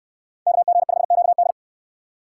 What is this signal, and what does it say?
Morse code, heard as a keyed tone, about 61 wpm, 700 Hz. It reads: KG5QD